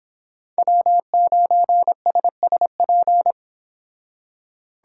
Morse code, heard as a keyed tone, about 26 wpm, 700 Hz. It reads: W9SSP